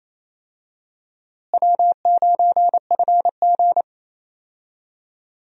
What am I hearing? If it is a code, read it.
Morse code, heard as a keyed tone, about 28 wpm, 700 Hz. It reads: W9FG
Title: USA Call Signs - 4 Characters - Repeated 2x - 28wpm